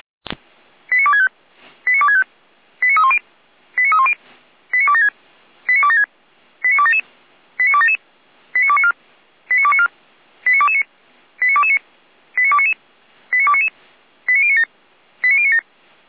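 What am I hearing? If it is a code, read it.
Five-tone selective call (ZVEI), heard as quick runs of five tones, tones of about 70 ms. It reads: 89265 89265 89310 89310 89276 89276 89277 89277 89294 89294 89229 89229 89299 89299 89097 89097
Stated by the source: ZVEI five-tone call 89075 received on an SDR.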